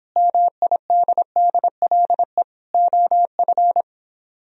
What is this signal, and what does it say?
Morse code, heard as a keyed tone, about 26 wpm, 700 Hz. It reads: MIDDLE OF